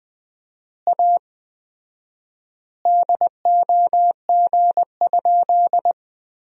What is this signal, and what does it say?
Morse code, heard as a keyed tone, about 20 wpm, 700 Hz. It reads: A DOG?
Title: 4x Word-Spacing - Sentences from Top 400 Words - 20wpm